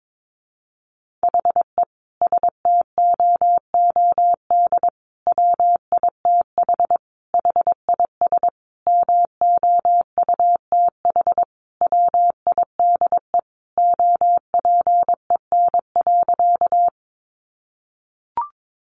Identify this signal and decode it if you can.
Morse code, heard as a keyed tone, about 22 wpm, 700 Hz. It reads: HE STOOD WITH HIS MOUTH WIDE OPEN.
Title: Sentences from Top 1500 Words 22wpm